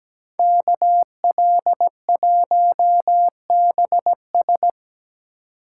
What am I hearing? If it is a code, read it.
Morse code, heard as a keyed tone, about 19 wpm, 700 Hz. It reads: KL1BS
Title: Speed Racing - USA 2x2 Call Signs 17wpm